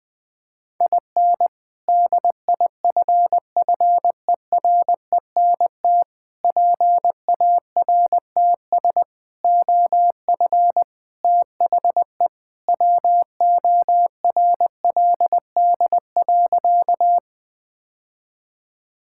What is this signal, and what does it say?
Morse code, heard as a keyed tone, about 20 wpm, 700 Hz. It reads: IN DIFFERENT PARTS OF THE WORLD.
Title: Sentences with 3-Word Phrases - 20wpm